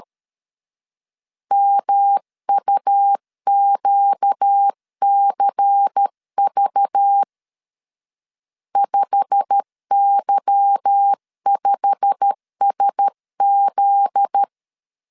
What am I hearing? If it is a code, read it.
Morse code, heard as a keyed tone, about 13 wpm, 800 Hz. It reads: MUQCV 5Y5SZ
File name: LEZIONI / 8 YJ45 / 8 YJ45 12wpm 2151